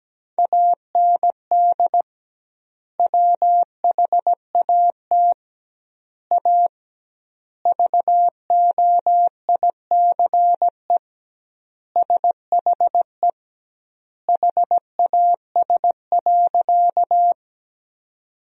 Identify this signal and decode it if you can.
Morse code, heard as a keyed tone, about 17 wpm, 700 Hz. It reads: AND WHAT A VOICE SHE HAS.
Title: 2x Word-Spacing - Sentences from Top 500 Words - 17wpm